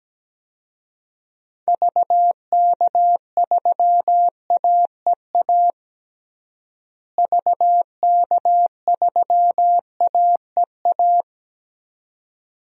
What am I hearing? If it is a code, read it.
Morse code, heard as a keyed tone, about 17 wpm, 700 Hz. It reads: VK3AEA VK3AEA
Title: DXCC-100 Call Signs - 6 Characters - Repeated 2x - 17wpm